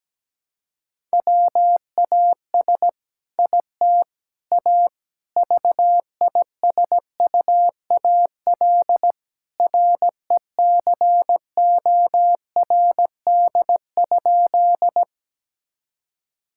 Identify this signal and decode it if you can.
Morse code, heard as a keyed tone, about 17 wpm, 700 Hz. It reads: WAS IT A VISUAL RECORD?
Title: Sentences from Top 1000 Words - Review Part 6 - 17wpm